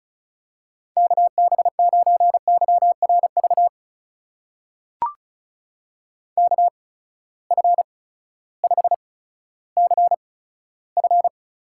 Morse code, heard as a keyed tone, about 36 wpm, spaced as for 16 wpm, 700 Hz. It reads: KB9YRV KF5CF
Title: USA Call Signs - 6 Characters - 8x Character Spacing - 35wpm